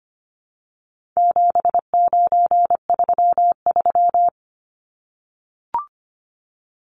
Morse code, heard as a keyed tone, about 25 wpm, 700 Hz. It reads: 7933
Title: Numbers - 4-Digit 25wpm